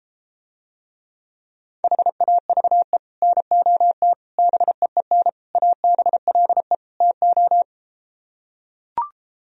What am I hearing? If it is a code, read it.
Morse code, heard as a keyed tone, about 33 wpm, 700 Hz. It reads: HAVE NOT BEEN ABLE TO